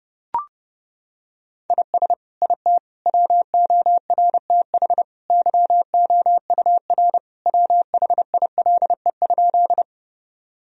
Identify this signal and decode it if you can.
Morse code, heard as a keyed tone, about 29 wpm, 700 Hz. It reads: IS IT WORTH YOUR WHILE?
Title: Sentences from Top 1000 Words - Review Part 1 - 30wpm